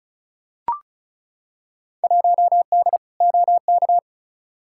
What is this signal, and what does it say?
Morse code, heard as a keyed tone, about 36 wpm, 700 Hz. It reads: 1D OK
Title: Field Day Exchanges - 35wpm